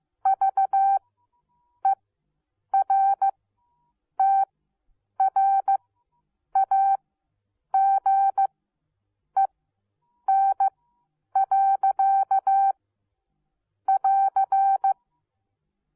Morse code, heard as a keyed tone, about 15 wpm, spaced as for 8 wpm, 780 Hz. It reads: VERTRAGEN. <AR>